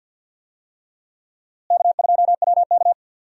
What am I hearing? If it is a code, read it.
Morse code, heard as a keyed tone, about 51 wpm, 700 Hz. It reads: K2WK
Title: USA Call Signs - 4 Characters - Repeated 3x - 50wpm